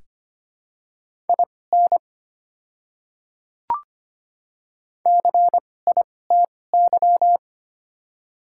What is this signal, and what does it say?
Morse code, heard as a keyed tone, about 25 wpm, spaced as for 18 wpm, 700 Hz. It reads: IN CITY
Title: 2x Character Spacing - Top 300 Words - 25wpm